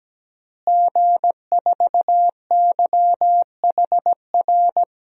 Morse code, heard as a keyed tone, about 17 wpm, 700 Hz. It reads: G4YHR
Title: DXCC-100 Call Signs - 5 Characters - Repeated 3x - 17wpm